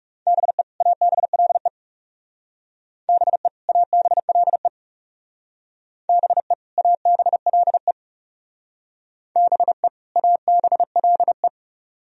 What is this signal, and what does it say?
Morse code, heard as a keyed tone, about 38 wpm, 700 Hz. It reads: BE ABLE BE ABLE BE ABLE BE ABLE